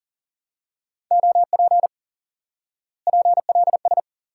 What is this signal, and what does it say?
Morse code, heard as a keyed tone, about 40 wpm, 700 Hz. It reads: OP PLS
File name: IQE Course 3 - Lesson 89 - YAGI - Two Elements - 40wpm